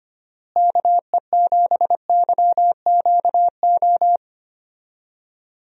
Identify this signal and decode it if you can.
Morse code, heard as a keyed tone, about 25 wpm, 700 Hz. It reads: KE7YQO
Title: USA Call Signs - 6 Characters - Repeated 2x - 25wpm